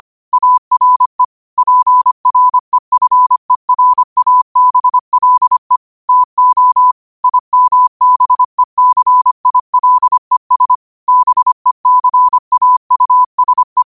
Morse code, heard as a keyed tone, about 25 wpm, 1000 Hz. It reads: ARE PREFERABLE TO IMBECILES BECAUSE
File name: qotdfiles / 25wpm